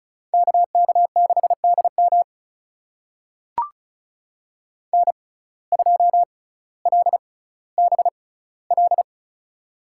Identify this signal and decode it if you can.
Morse code, heard as a keyed tone, about 35 wpm, spaced as for 18 wpm, 700 Hz. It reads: KK6DM N2LBL
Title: USA Call Signs - 5 Characters - 6x Character Spacing - 35wpm